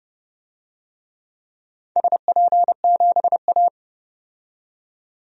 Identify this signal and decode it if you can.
Morse code, heard as a keyed tone, about 30 wpm, 700 Hz. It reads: SP7A
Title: DXCC-100 Call Signs - 4 Characters - 30wpm